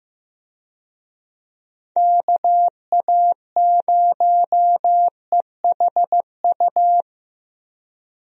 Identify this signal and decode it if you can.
Morse code, heard as a keyed tone, about 15 wpm, 700 Hz. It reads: KA0EHU
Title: USA Call Signs - 6 Characters - Repeated 3x - 15wpm